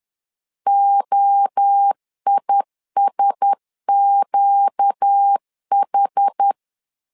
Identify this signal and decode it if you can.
Morse code, heard as a keyed tone, about 10 wpm, 800 Hz. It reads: OISQH
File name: LEZIONI / 4 Q1S 10wpm 2133